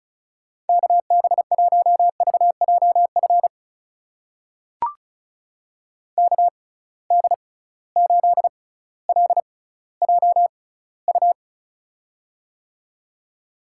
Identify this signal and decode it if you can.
Morse code, heard as a keyed tone, about 36 wpm, spaced as for 18 wpm, 700 Hz. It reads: KB1VJF KD8LJU
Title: USA Call Signs - 6 Characters - 6x Character Spacing - 35wpm